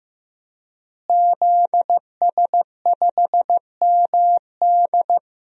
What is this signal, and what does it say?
Morse code, heard as a keyed tone, about 15 wpm, 700 Hz. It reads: ZS5MD
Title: DXCC-100 Call Signs - 5 Characters - Repeated 2x - 15wpm